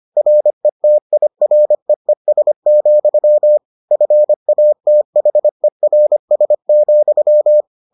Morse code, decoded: RETIREES, FATHERS,

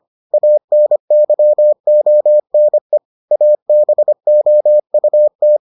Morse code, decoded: ANYONE ABOUT

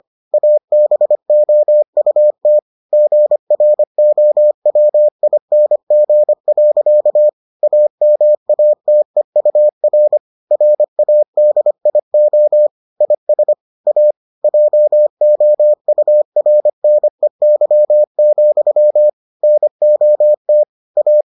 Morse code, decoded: ABOUT GROWING. AMATEUR RADIO IS A JOURNEY, NOT A

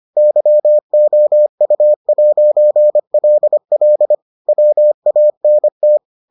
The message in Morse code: YOU'LL WANT

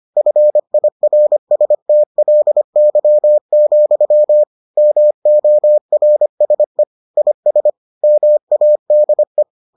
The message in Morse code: FIRSTLY, MORSE IS MADE

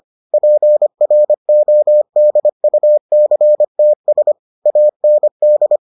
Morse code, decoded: PRODUCTS AND

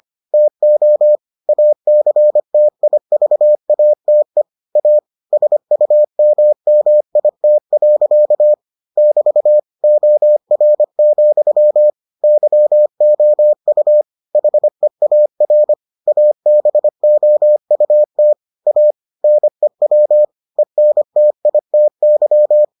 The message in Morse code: TO ACTIVATE A SUMMIT. <BT> OR, YOU HEAR ABOUT A NEW ENTITY

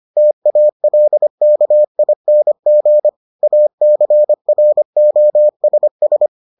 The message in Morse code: TALKING ACROSS

